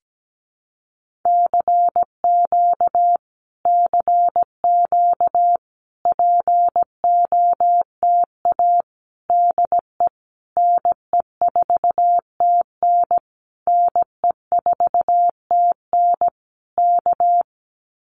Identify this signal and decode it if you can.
Morse code, heard as a keyed tone, about 17 wpm, 700 Hz. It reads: CQ CQ POTA DE NE4TN NE4TN K
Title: Common QSO Phrases Encore 17wpm